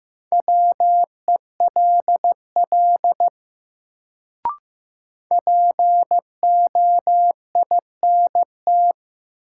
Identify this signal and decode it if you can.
Morse code, heard as a keyed tone, about 15 wpm, 700 Hz. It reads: WELL POINT
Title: Sentences from Top 200 Words - New Word Drill - 15wpm